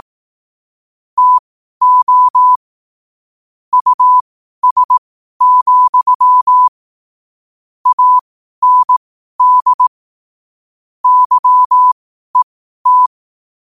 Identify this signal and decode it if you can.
Morse code, heard as a keyed tone, about 18 wpm, spaced as for 12 wpm, 1000 Hz. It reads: TO US, AND YET
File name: qotdfiles / 12wpm